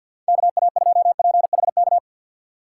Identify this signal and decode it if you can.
Morse code, heard as a keyed tone, about 50 wpm, 700 Hz. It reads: KA2PHK
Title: USA Call Signs - 4x Character Spacing - 50wpm